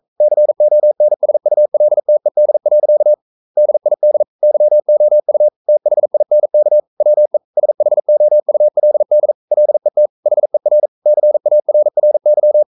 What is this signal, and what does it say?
Morse code, heard as a keyed tone, about 42 wpm, 600 Hz. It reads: CONSULTED. DID YOU THINK WE SHOULD LET HER CARRY
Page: Romanen Never af Ken Follet på engelsk.